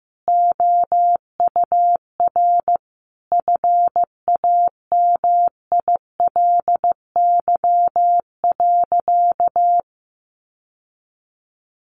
Morse code, heard as a keyed tone, about 15 wpm, 700 Hz. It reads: OUR FAMILY.
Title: Sentences from Top 1300 Words 15wpm